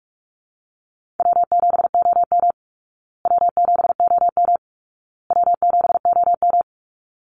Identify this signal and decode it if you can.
Morse code, heard as a keyed tone, about 44 wpm, 700 Hz. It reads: W7OM W7OM W7OM
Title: Easy Call Signs Repeated 3x - 45wpm